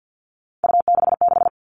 Morse code, heard as a keyed tone, about 50 wpm, 700 Hz. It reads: U66
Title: Three Challenging Letter-Number Combinations 50wpm